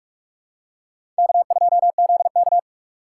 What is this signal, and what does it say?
Morse code, heard as a keyed tone, about 42 wpm, 700 Hz. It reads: K2ZK
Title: USA Call Signs - 4 Characters - 4x Character Spacing - 45wpm